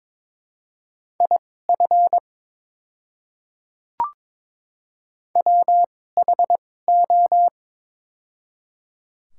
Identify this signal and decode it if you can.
Morse code, heard as a keyed tone, about 22 wpm, spaced as for 16 wpm, 700 Hz. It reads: IF WHO
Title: 2x Character Spacing - Top 200 Words - 22wpm